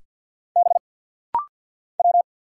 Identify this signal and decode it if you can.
Morse code, heard as a keyed tone, about 48 wpm, 700 Hz. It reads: B W
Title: Single Letter-Number - Rapid-Fire 50wpm